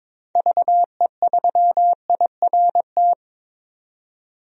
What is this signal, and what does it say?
Morse code, heard as a keyed tone, about 22 wpm, 700 Hz. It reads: VE3IRT
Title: DXCC-100 Call Signs - 6 Characters - 22wpm